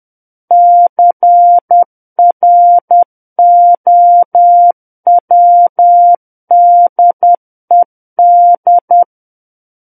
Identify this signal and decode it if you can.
Morse code, heard as a keyed tone, about 10 wpm, 700 Hz. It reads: CROWDED
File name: Book One / 01-Chapter 1 - 10 WPM